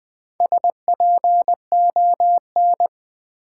Morse code, heard as a keyed tone, about 20 wpm, 700 Hz. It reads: SPON
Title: Build A Word - 4 Letters At A Time - 11-Letter Words - 20wpm